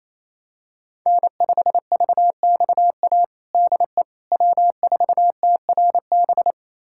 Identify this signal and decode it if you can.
Morse code, heard as a keyed tone, about 28 wpm, 700 Hz. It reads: N5VXA DE W4TRB
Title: USA Callsign DE Callsign - 5 Characters - 28wpm